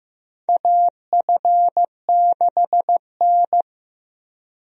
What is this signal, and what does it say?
Morse code, heard as a keyed tone, about 15 wpm, 700 Hz. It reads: AF6N
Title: USA Call Signs - 4 Characters - Repeated 2x - 15wpm